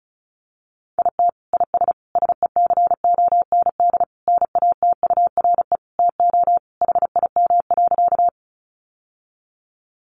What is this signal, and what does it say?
Morse code, heard as a keyed tone, about 34 wpm, 700 Hz. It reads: IT IS SECOND NATURE TO HIM.
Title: Sentences from Top 600 Words 35wpm